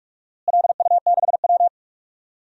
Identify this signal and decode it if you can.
Morse code, heard as a keyed tone, about 44 wpm, 700 Hz. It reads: LU6W
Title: DXCC-100 Call Signs - 4 Characters - Repeated 3x - 45wpm